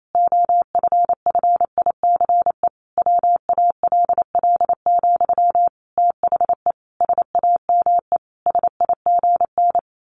Morse code, heard as a keyed tone, about 28 wpm, 700 Hz. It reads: OFFICE WALL, THE SAME SIGN
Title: A Sound Of Thunder 28wpm